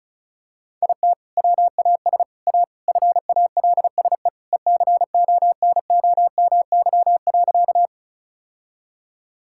Morse code, heard as a keyed tone, about 34 wpm, 700 Hz. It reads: IT WAS A FALSE ECONOMY.
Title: Sentences from Top 1600 Words 35wpm